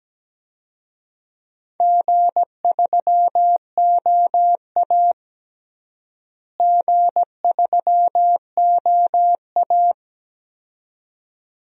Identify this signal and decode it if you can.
Morse code, heard as a keyed tone, about 17 wpm, 700 Hz. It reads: G3OA G3OA
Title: DXCC-100 Call Signs - 4 Characters - Repeated 2x - 17wpm